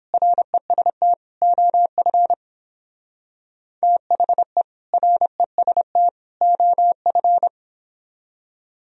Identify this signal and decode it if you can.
Morse code, heard as a keyed tone, about 26 wpm, 700 Hz. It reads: REST OF THE REST OF
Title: Speed-Racing - Sets of 3 Words 22wpm